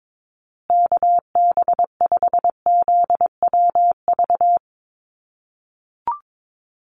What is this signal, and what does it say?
Morse code, heard as a keyed tone, about 22 wpm, 700 Hz. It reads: KB5ZWV
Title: USA Call Signs Repeated 3x 22wpm